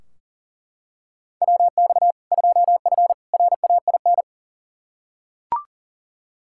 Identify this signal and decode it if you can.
Morse code, heard as a keyed tone, about 40 wpm, 700 Hz. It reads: WX 2F RAIN
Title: K1USN SST QSO Practice 40wpm